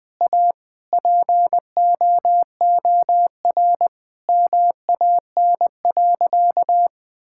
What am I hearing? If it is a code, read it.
Morse code, heard as a keyed tone, about 20 wpm, 700 Hz. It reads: A POOR MAN.
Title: Sentences from Top 1000 Words - Review Part 5 - 20wpm